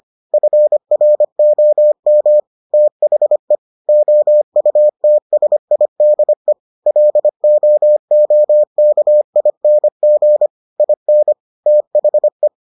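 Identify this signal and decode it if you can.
Morse code, heard as a keyed tone, about 25 wpm, 600 Hz. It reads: FROM THE OUTSIDE LOOKING IN THE
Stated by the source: Morse edition of the Foundations of Amateur Radio episodes.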